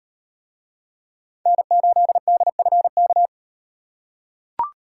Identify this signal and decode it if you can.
Morse code, heard as a keyed tone, about 38 wpm, 700 Hz. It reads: N8DFK